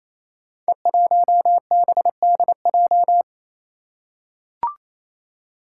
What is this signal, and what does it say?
Morse code, heard as a keyed tone, about 28 wpm, 700 Hz. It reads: E1BDJ
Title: Code Groups - 5 Letter:Numbers - 28wpm